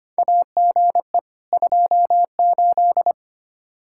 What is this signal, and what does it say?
Morse code, heard as a keyed tone, about 25 wpm, 700 Hz. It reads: AGE 28
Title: Practice QSO - Exchange 3 - More Info - 25wpm